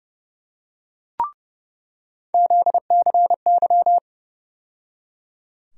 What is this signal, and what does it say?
Morse code, heard as a keyed tone, about 30 wpm, 700 Hz. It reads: ZCY
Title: USA Call Sign Postfixes - 3 Letters 30wpm